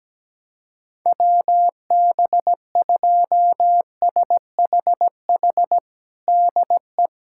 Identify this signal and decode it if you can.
Morse code, heard as a keyed tone, about 17 wpm, 700 Hz. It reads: WB2SHH DE
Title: USA Callsign DE Callsign - 6 Characters - 17wpm